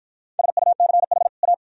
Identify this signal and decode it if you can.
Morse code, heard as a keyed tone, about 53 wpm, 700 Hz. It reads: SUCH A